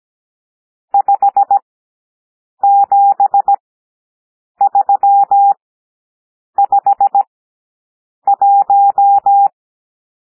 Morse code, heard as a keyed tone, about 17 wpm, spaced as for 7 wpm, 800 Hz. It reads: 57351
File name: SOUND NUMBERS 17 wpm